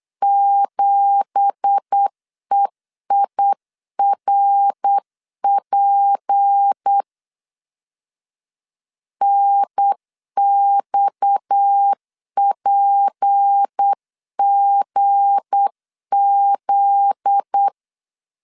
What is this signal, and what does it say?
Morse code, heard as a keyed tone, about 8 wpm, 800 Hz. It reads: 7EIRP NXPGZ
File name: CW / DET / 1 9 1500C S A 8wpm 5637